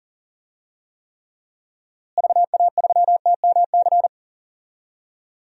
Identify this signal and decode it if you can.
Morse code, heard as a keyed tone, about 40 wpm, 700 Hz. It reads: VA3TMC
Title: DXCC-100 Call Signs - 6 Characters - Repeated 2x - 40wpm